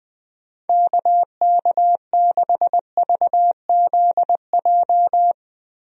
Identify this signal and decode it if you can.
Morse code, heard as a keyed tone, about 20 wpm, 700 Hz. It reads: KK6VZJ
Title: USA Call Signs - 6 Characters - Repeated 2x - 20wpm